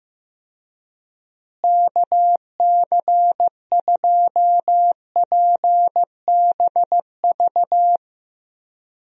Morse code, heard as a keyed tone, about 15 wpm, 700 Hz. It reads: KC2PBV
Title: USA Call Signs - 6 Characters - Repeated 3x - 15wpm